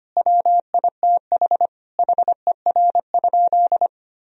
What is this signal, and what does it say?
Morse code, heard as a keyed tone, about 25 wpm, 700 Hz. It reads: WITH HER?
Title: Sentences from Top 1000 Words - Review Part 4 - 25wpm